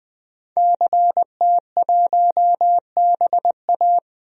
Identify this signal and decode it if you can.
Morse code, heard as a keyed tone, about 20 wpm, 700 Hz. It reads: CT1BA
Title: DXCC-100 Call Signs - 5 Characters - Repeated 3x - 20wpm